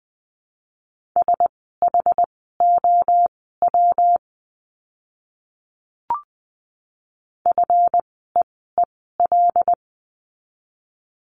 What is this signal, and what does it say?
Morse code, heard as a keyed tone, about 20 wpm, spaced as for 14 wpm, 700 Hz. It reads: SHOW FEEL
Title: 244 Lesson 39f - Letter z - Words 20wpm